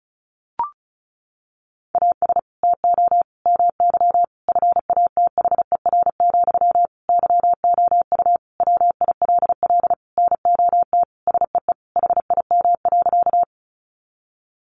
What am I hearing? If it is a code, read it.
Morse code, heard as a keyed tone, about 34 wpm, 700 Hz. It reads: AS TO MY FATHER, YOU WILL NOT SEE HIM.
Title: Sentences from Top 600 Words 35wpm